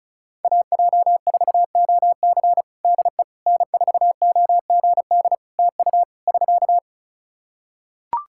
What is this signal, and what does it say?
Morse code, heard as a keyed tone, about 36 wpm, 700 Hz. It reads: AJ4OC DE N4OGD TU <SK>
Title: Practice QSO - Exchange 4 - Ending - 35wpm